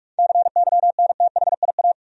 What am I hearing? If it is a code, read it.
Morse code, heard as a keyed tone, about 43 wpm, 700 Hz. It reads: CYNTHIA
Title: USA Common Names - 45wpm